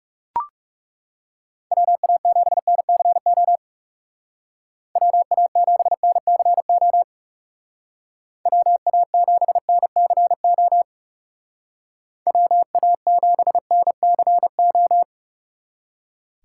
Morse code, heard as a keyed tone, about 36 wpm, 700 Hz. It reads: WA7NCO WA7NCO WA7NCO WA7NCO
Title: Speed Racing - USA Call Signs 30wpm